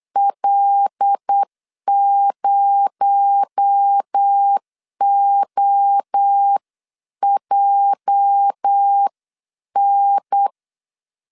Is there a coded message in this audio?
Yes: Morse code from 0.2 to 10.5 s, heard as a keyed tone, about 8 wpm, 800 Hz. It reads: L0OJN